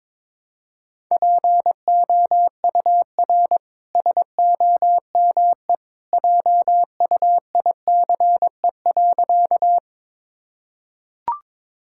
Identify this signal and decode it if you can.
Morse code, heard as a keyed tone, about 22 wpm, 700 Hz. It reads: POUR SOME JUICE.
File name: practice / 22wpm / Easy Sentences with Hints - 22wpm